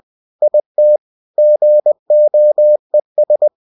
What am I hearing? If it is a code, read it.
Morse code, heard as a keyed tone, about 20 wpm, 600 Hz. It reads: IT GOES
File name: cw-books / winnie-the-pooh / winnie-20wpm-page-0058